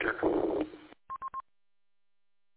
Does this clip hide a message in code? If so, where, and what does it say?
No, only speech.